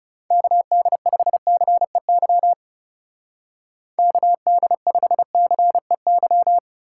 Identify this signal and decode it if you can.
Morse code, heard as a keyed tone, about 37 wpm, 700 Hz. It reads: KD5CEY KD5CEY
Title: Speed Racing - USA 2x3 Call Signs 30wpm